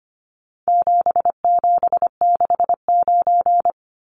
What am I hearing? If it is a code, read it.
Morse code, heard as a keyed tone, about 25 wpm, 700 Hz. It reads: 7769
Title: Numbers - 4-Digit 25wpm